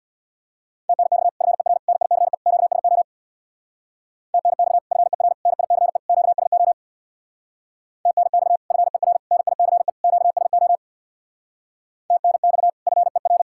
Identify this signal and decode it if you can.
Morse code, heard as a keyed tone, about 59 wpm, 700 Hz. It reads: TNX FER NICE QSO TNX FER NICE QSO TNX FER NICE QSO TNX FER
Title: Speed-Racing Common QSO Phrases - 50wpm